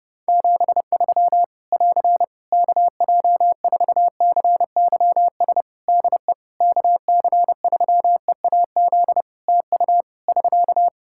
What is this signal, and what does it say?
Morse code, heard as a keyed tone, about 30 wpm, 700 Hz. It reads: 73 <AR> KJ4CYS DE KC3EAZ TU <SK>
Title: Practice QSO - Exchange 4 - Ending - 30wpm